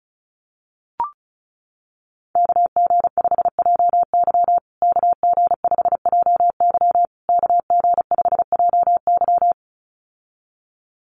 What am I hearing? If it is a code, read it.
Morse code, heard as a keyed tone, about 34 wpm, 700 Hz. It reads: KG5JY KG5JY KG5JY
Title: USA Call Signs Repeated 3x 35wpm